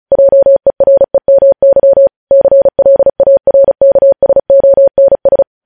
Morse code, heard as a keyed tone, about 36 wpm, 550 Hz. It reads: JEREMY CLARKSONS
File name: podcast / News headlines-1x-35x35